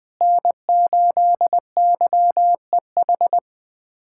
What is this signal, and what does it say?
Morse code, heard as a keyed tone, about 20 wpm, 700 Hz. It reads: N8YEH
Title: USA Callsign DE Callsign - 5 Characters - 20wpm